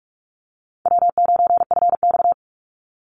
Morse code, heard as a keyed tone, about 44 wpm, 700 Hz. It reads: W9FX
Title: Easy Call Signs - 45wpm